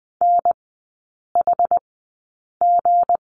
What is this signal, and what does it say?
Morse code, heard as a keyed tone, about 20 wpm, spaced as for 8 wpm, 700 Hz. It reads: NHG
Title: 178 Lesson 29e - Slash - Three Characters 20wpm